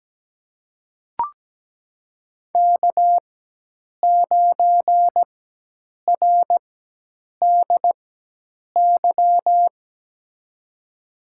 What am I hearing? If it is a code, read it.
Morse code, heard as a keyed tone, about 17 wpm, spaced as for 8 wpm, 700 Hz. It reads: K9RDY
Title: USA Call Signs - 5 Characters - 4x Character Spacing - 17wpm